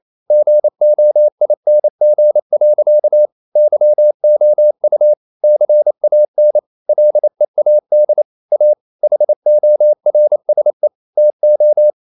Morse code, heard as a keyed tone, about 28 wpm, 600 Hz. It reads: GOING. YOU CAN LEAD A HORSE TO
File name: CW / Practice / EnglishSayings28